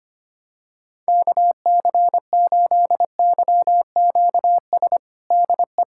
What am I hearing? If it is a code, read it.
Morse code, heard as a keyed tone, about 25 wpm, 700 Hz. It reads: KC8YQS DE
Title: USA Callsign DE Callsign 25wpm